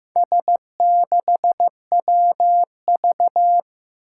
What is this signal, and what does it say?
Morse code, heard as a keyed tone, about 15 wpm, 700 Hz. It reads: S6WV
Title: Four Challenging Letter-Number Combinations 15wpm